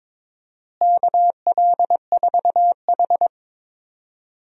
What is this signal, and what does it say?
Morse code, heard as a keyed tone, about 22 wpm, 700 Hz. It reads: KL4H
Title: USA Call Signs - 4 Characters - Repeated 3x - 22wpm